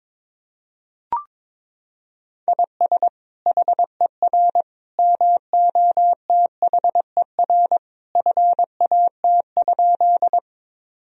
Morse code, heard as a keyed tone, about 22 wpm, 700 Hz. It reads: IS HER MOTHER FAT?
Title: Sentences from Top 1000 Words - Review Part 4 - 22wpm